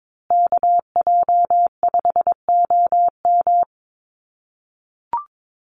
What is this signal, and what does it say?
Morse code, heard as a keyed tone, about 22 wpm, 700 Hz. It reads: KJ5OM
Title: USA Call Signs Repeated 2x 22wpm